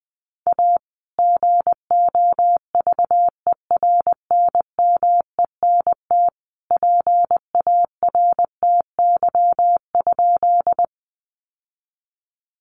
Morse code, heard as a keyed tone, about 20 wpm, 700 Hz. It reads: A GOVERNMENT PARTY?
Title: Sentences from Top 1400 Words 20wpm